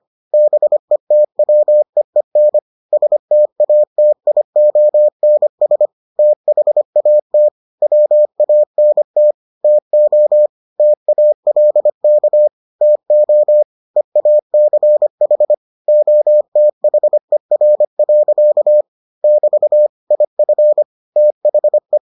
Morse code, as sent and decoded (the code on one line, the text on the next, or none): -... - .-- . -. / ... - .- - .. --- -. ... / - .... .- - / .-- .- -. - / - --- / - .- .-.. -.- / - --- / . .- -.-. .... / --- - .... .-. .-.-.- / -...- / .. ..-. / - .... .
BETWEEN STATIONS THAT WANT TO TALK TO EACH OTHER. <BT> IF THE